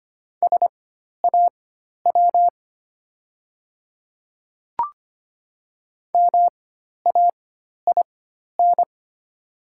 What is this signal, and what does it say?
Morse code, heard as a keyed tone, about 25 wpm, spaced as for 12 wpm, 690 Hz. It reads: SAW MAIN
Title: 4x Character Spacing - Top 300 Words - 25wpm